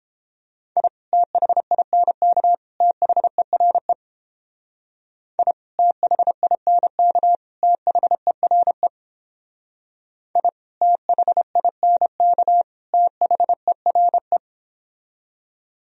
Morse code, heard as a keyed tone, about 29 wpm, 700 Hz. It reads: I THINK THERE I THINK THERE I THINK THERE